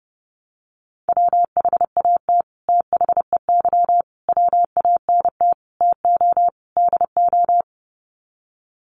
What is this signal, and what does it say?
Morse code, heard as a keyed tone, about 30 wpm, 700 Hz. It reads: WHAT THEY WANT TO DO
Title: Sets of 5 Words Encore2 30wpm